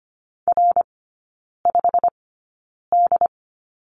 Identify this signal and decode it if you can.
Morse code, heard as a keyed tone, about 25 wpm, spaced as for 9 wpm, 700 Hz. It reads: R5D
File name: course / 25:10wpm / 093 Lesson 16e - Letter c - Three Characters 25wpm